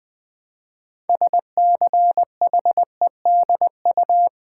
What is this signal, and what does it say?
Morse code, heard as a keyed tone, about 20 wpm, 700 Hz. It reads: SCHEDU
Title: Build A Word - 2 Letters At A Time - 8-Letter Words - 20wpm